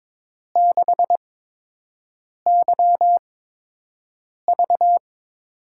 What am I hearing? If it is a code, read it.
Morse code, heard as a keyed tone, about 22 wpm, spaced as for 6 wpm, 700 Hz. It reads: 6YV
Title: USA Call Signs - 5 Characters - 8x Character Spacing - 22wpm